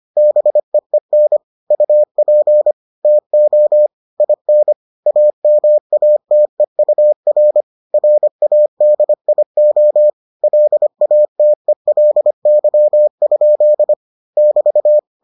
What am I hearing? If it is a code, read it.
Morse code, heard as a keyed tone, about 25 wpm, 600 Hz. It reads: BEEN UP TO IN AMATEUR RADIO LATELY? <BT>